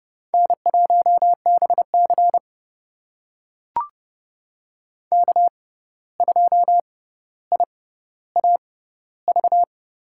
Morse code, heard as a keyed tone, about 30 wpm, spaced as for 14 wpm, 700 Hz. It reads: N1BC K2IAV
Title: USA Call Signs - 6x Character Spacing - 30wpm